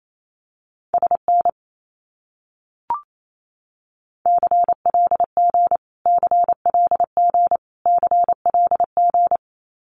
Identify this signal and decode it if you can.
Morse code, heard as a keyed tone, about 28 wpm, 700 Hz. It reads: SN CLG CLG CLG
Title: Abbreviations Repeated 3x - 28wpm